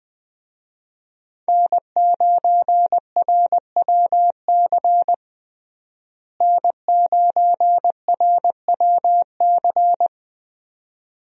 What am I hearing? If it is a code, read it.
Morse code, heard as a keyed tone, about 20 wpm, 700 Hz. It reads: N9RWC N9RWC
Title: USA Call Signs - 5 Characters - Repeated 3x - 20wpm